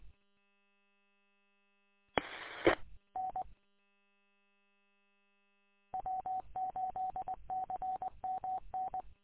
Morse code, heard as a keyed tone, about 20 wpm, 750 Hz. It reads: E E N W8CMN
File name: w8cmn- -combined